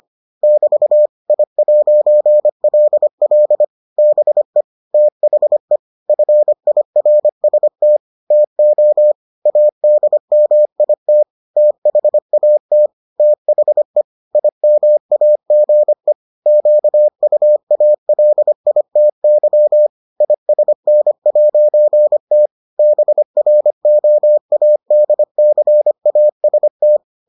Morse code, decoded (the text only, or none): <BT> I'LL BE THE FIRST TO ADMIT THAT THE IMAGE QUALITY ISN'T BROADCAST